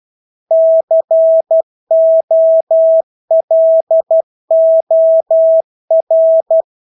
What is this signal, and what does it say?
Morse code, heard as a keyed tone, about 12 wpm, 650 Hz. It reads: COLOR
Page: Engelske ord 12 wpm